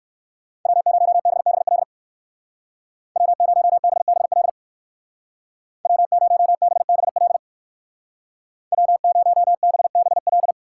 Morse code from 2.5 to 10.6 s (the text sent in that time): W0BBL W0BBL W0BBL